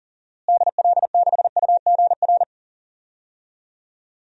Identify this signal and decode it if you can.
Morse code, heard as a keyed tone, about 40 wpm, 700 Hz. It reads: DL6UGR
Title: DXCC-100 Call Signs - 6 Characters - Repeated 3x - 40wpm